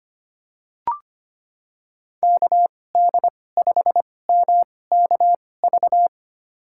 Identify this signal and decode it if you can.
Morse code, heard as a keyed tone, about 25 wpm, spaced as for 18 wpm, 700 Hz. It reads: KD5MKV
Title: USA Call Signs - 6 Characters - 2x Character Spacing - 25wpm